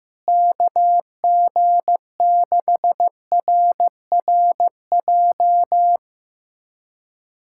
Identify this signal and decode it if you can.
Morse code, heard as a keyed tone, about 16 wpm, 700 Hz. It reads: KG6RRJ
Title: USA Callsign DE Callsign - 6 Characters - 15wpm